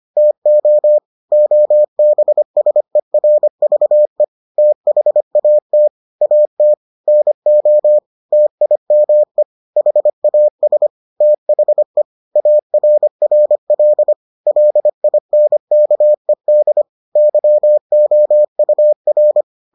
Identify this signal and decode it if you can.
Morse code, heard as a keyed tone, about 25 wpm, 600 Hz. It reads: TO OBSERVE THAT AT NO TIME HAS THE ARRL LINKED YOUR